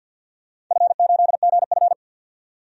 Morse code, heard as a keyed tone, about 52 wpm, 700 Hz. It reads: F8GF
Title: DXCC-100 Call Signs - 4 Characters - 50wpm